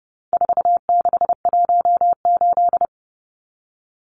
Morse code, heard as a keyed tone, about 30 wpm, 700 Hz. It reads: 4618
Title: Numbers - 4-Digit 30wpm